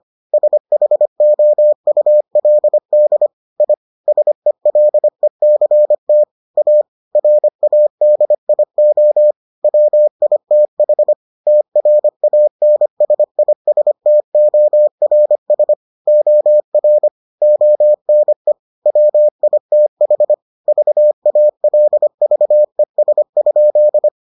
Morse code, heard as a keyed tone, about 25 wpm, 600 Hz. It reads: SHOULD I SELECT A RADIO WITH TRANSISTORS OR ONE WITH VALVES?